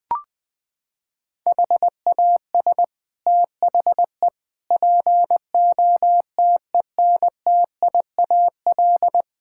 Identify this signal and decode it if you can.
Morse code, heard as a keyed tone, about 20 wpm, 700 Hz. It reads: HAS THE POTENTIAL
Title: Sets of 6 Words - 20wpm